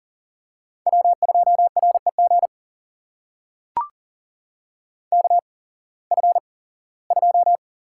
Morse code, heard as a keyed tone, about 40 wpm, spaced as for 21 wpm, 700 Hz. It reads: W2REG KF2
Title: USA Call Signs - 5 Characters - 8x Character Spacing - 40wpm